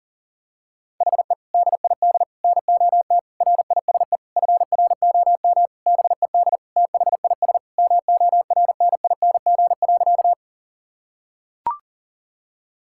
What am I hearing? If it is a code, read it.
Morse code, heard as a keyed tone, about 37 wpm, 700 Hz. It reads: HE DID NOT RISE FROM BED THIS MORNING.
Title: Sentences from Top 1000 Words - Review Part 6 - 40wpm